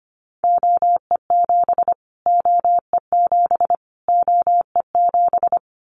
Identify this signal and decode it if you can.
Morse code, heard as a keyed tone, about 25 wpm, 700 Hz. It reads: OE7 OE7 OE7
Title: Callsign Prefixes for Top 25 DXCC - Repeated 3x - 25wpm